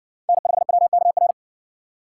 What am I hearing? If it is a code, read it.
Morse code, heard as a keyed tone, about 60 wpm, 700 Hz. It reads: N5WKR